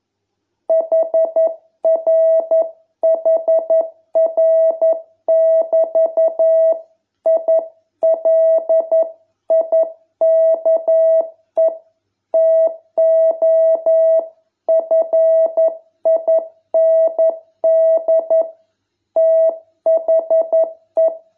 Morse code, heard as a keyed tone, about 11 wpm, 660 Hz. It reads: HRHR<BT> I LIKE TO FIND THE